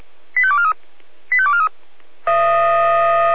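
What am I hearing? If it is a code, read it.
Five-tone selective call (ZVEI), heard as quick runs of five tones, tones of about 80 ms. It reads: 85243 85243
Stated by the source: ZVEI five-tone call 85882 received by SDR.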